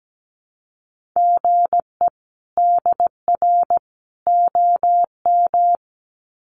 Morse code, heard as a keyed tone, about 17 wpm, 700 Hz. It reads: GE DR OM
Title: Common QSO Phrases 17wpm